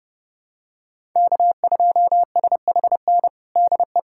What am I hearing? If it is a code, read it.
Morse code, heard as a keyed tone, about 30 wpm, 700 Hz. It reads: K2SHN DE
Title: USA Callsign DE Callsign - 5 Characters - 30wpm